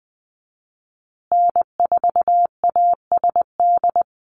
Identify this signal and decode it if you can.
Morse code, heard as a keyed tone, about 20 wpm, 700 Hz. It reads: N4ASD
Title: USA Call Signs Repeated 2x 20wpm